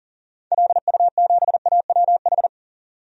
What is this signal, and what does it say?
Morse code, heard as a keyed tone, about 39 wpm, 700 Hz. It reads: LU7AWH
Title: DXCC-100 Call Signs - 6 Characters - 40wpm